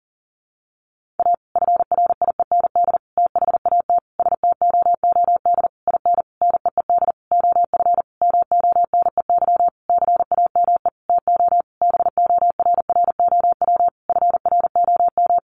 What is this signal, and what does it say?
Morse code, heard as a keyed tone, about 40 wpm, 700 Hz. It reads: A FRIEND THAT STOOD IN NEED OF MONEY CAME TO BORROW FROM
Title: Sentences from Top 1500 Words 40wpm